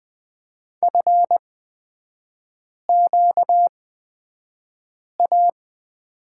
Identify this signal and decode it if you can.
Morse code, heard as a keyed tone, about 20 wpm, spaced as for 5 wpm, 700 Hz. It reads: FQA
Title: 250 Lesson 40e - BK - Three Characters 20wpm